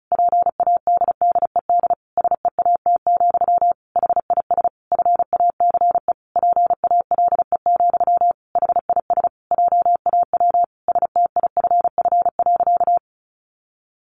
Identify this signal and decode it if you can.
Morse code, heard as a keyed tone, about 34 wpm, 700 Hz. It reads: PADDED SEAT, HIS FACE PALE, HIS JAW STIFF.
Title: A Sound Of Thunder 35wpm